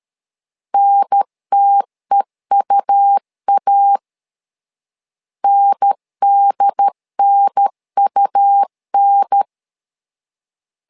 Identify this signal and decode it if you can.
Morse code, heard as a keyed tone, about 13 wpm, 800 Hz. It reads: NTEUA NDNUN
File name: LEZIONI / 2 UHDN / 2 UHDN 12wpm 1456